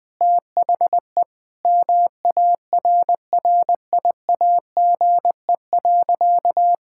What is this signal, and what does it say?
Morse code, heard as a keyed tone, about 20 wpm, 700 Hz. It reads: THE MARRIAGE.
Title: Sentences from Top 1000 Words - Review Part 5 - 20wpm